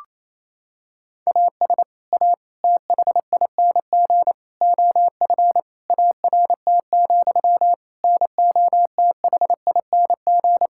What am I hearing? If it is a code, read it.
Morse code, heard as a keyed tone, about 28 wpm, 700 Hz. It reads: AS A THING OF ART, NOTHING
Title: Sentences from Top 1000 Words - Review Part 6 - 28wpm